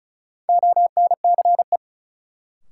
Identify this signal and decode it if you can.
Morse code, heard as a keyed tone, about 36 wpm, 700 Hz. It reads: ONCE
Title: Sets of 4 Words - New Word Drill - 35wpm